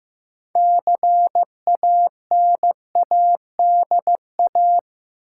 Morse code, heard as a keyed tone, about 15 wpm, 700 Hz. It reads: CANADA
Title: Sets of 5 Words - New Word Drill - 15wpm